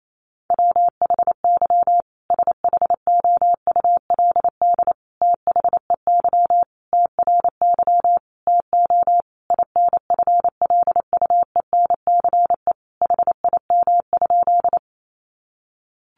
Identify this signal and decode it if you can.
Morse code, heard as a keyed tone, about 28 wpm, 700 Hz. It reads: WHY SHOULD THEY TRY TO INFLUENCE HIM?
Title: Sentences from Top 1100 Words 28wpm